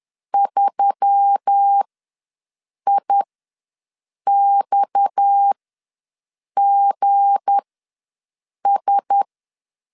Morse code, heard as a keyed tone, about 10 wpm, spaced as for 6 wpm, 800 Hz. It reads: 3IXGS
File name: LEZIONI / 7 GF9Z / 7 G ws 10wpm 3843